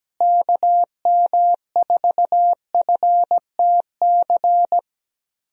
Morse code, heard as a keyed tone, about 17 wpm, 700 Hz. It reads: KM4FTC
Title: USA Call Signs - 6 Characters - Repeated 3x - 17wpm